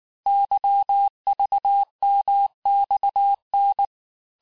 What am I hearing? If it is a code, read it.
Morse code, heard as a keyed tone, about 19 wpm, 800 Hz. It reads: YVMXN
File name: cours-du-lundi files / cours-1595-teste-CHIFFRE-vitesse18-MPM